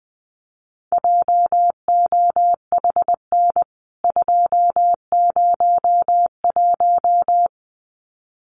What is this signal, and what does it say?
Morse code, heard as a keyed tone, about 20 wpm, 700 Hz. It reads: JOHN 201
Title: CWT Exchange 20wpm